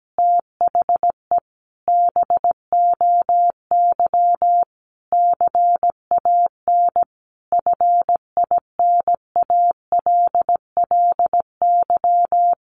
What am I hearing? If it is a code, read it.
Morse code, heard as a keyed tone, about 17 wpm, 700 Hz. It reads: THE BOY CAN FINALLY
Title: Additional Easy Phrases 17wpm